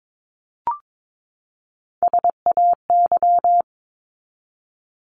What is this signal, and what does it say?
Morse code, heard as a keyed tone, about 22 wpm, 700 Hz. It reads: SAY
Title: Top 200 Words 22wpm